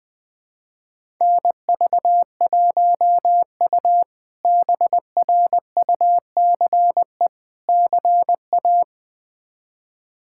Morse code, heard as a keyed tone, about 21 wpm, 700 Hz. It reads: NV1U BRUCE CA
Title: K1USN SST QSO Practice - 20wpm